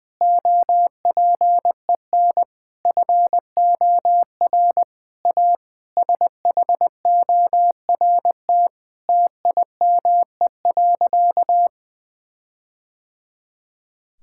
Morse code, heard as a keyed tone, about 20 wpm, 700 Hz. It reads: OPEN FOR A SHORT TIME.
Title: Sentences from Top 1600 Words 20wpm